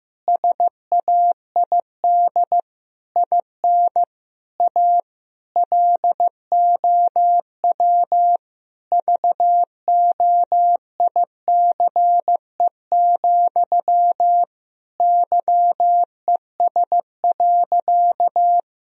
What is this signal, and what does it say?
Morse code, heard as a keyed tone, about 15 wpm, 700 Hz. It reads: SAID IN A LOW VOICE, YES.
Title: Sentences from Top 1000 Words - Review Part 1 - 15wpm